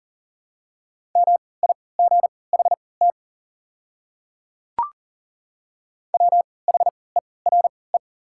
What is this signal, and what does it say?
Morse code, heard as a keyed tone, about 40 wpm, spaced as for 23 wpm, 700 Hz. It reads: MIGHT WHERE
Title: 3x Character Spacing - Top 300 Words - 40wpm